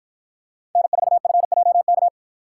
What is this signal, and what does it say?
Morse code, heard as a keyed tone, about 50 wpm, 700 Hz. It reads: N4FJK